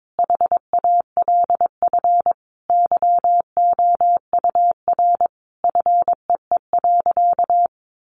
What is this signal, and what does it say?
Morse code, heard as a keyed tone, about 22 wpm, 700 Hz. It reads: HALF YOUR FEE.
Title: A Sound Of Thunder 22wpm